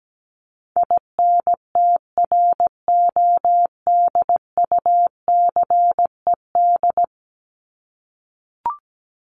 Morse code, heard as a keyed tone, about 17 wpm, 700 Hz. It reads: INTRODUCED
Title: ED Suffix - Top 100 Words with Base Words 17wpm